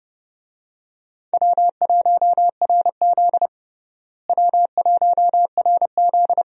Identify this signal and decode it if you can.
Morse code, heard as a keyed tone, about 30 wpm, 700 Hz. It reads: W1RZ W1RZ
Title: USA Call Signs - 4 Characters - Repeated 2x - 30wpm